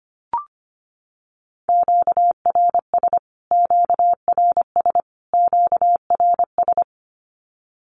Morse code, heard as a keyed tone, about 25 wpm, 700 Hz. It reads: QRS QRS QRS
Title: Q-Codes Repeated 3x 25wpm